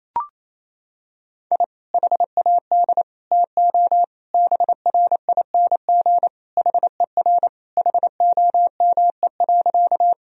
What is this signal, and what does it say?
Morse code, heard as a keyed tone, about 28 wpm, 700 Hz. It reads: I HAD TO BRING HER HOME.
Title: Sentences from Top 1000 Words - Review Part 3 - 28wpm